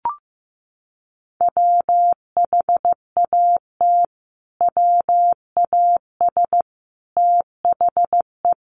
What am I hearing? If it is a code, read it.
Morse code, heard as a keyed tone, about 15 wpm, 700 Hz. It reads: WHAT WAS THE
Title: Sentences from Top 1400 Words 15wpm